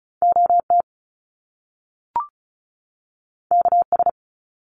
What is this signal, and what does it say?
Morse code, heard as a keyed tone, about 36 wpm, 700 Hz. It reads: OT KS
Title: Top Two Letter Combinations in Words 35wpm